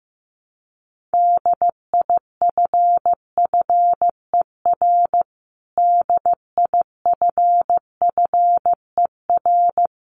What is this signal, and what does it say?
Morse code, heard as a keyed tone, about 15 wpm, 700 Hz. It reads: DIFFER DIFFER
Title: Top 500 Words Repeated 15wpm